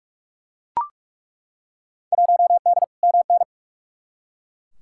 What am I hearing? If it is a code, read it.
Morse code, heard as a keyed tone, about 42 wpm, 700 Hz. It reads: 1D MN
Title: Field Day Exchanges - 45wpm